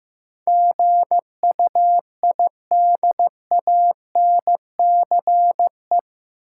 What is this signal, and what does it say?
Morse code, heard as a keyed tone, about 15 wpm, 700 Hz. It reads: GUIDANCE
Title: Sentences from Top 1600 Words - New Word Drill Alt - 15wpm